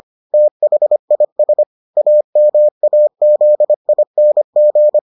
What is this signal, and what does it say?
Morse code, heard as a keyed tone, about 25 wpm, 600 Hz. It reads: THIS AMAZING